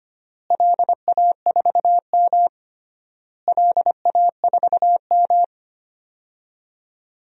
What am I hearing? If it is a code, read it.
Morse code, heard as a keyed tone, about 25 wpm, 700 Hz. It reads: LA4M LA4M
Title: DXCC-100 Call Signs - 4 Characters - Repeated 3x - 25wpm